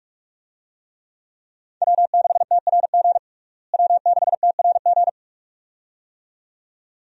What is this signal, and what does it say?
Morse code, heard as a keyed tone, about 46 wpm, 700 Hz. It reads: W6TRG W6TRG
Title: USA Call Signs - 5 Characters - Repeated 2x - 45wpm